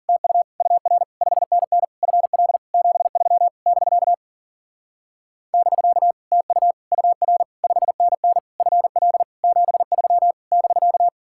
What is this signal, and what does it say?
Morse code, heard as a keyed tone, about 52 wpm, 700 Hz. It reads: TU UR 5NN FL 73 <BK> <BK> TU UR 5NN FL 73 <BK>